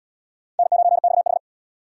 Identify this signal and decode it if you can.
Morse code, heard as a keyed tone, about 67 wpm, 700 Hz. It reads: N9XH